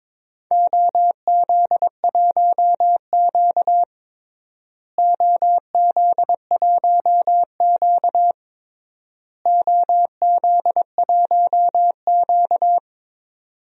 Morse code, heard as a keyed tone, about 22 wpm, 700 Hz. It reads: OZ1Q OZ1Q OZ1Q
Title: DXCC-100 Call Signs - 4 Characters - Repeated 3x - 22wpm